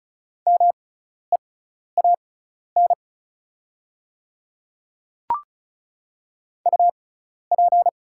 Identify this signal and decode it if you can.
Morse code, heard as a keyed tone, about 36 wpm, spaced as for 12 wpm, 690 Hz. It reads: MEAN UP
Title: 6x Character Spacing - Top 200 Words - 35wpm